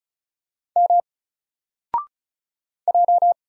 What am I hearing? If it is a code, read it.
Morse code, heard as a keyed tone, about 34 wpm, 700 Hz. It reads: M J
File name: Single-Letter-Number-35wpm